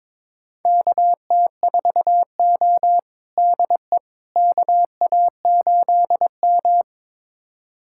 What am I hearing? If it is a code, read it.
Morse code, heard as a keyed tone, about 21 wpm, 700 Hz. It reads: KT4O DE KA8M
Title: USA Callsign DE Callsign - 4 Characters - 22wpm